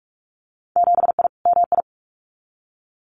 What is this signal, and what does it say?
Morse code, heard as a keyed tone, about 46 wpm, 700 Hz. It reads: 7I MI
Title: Winter Field Day Exchanges Encore 45wpm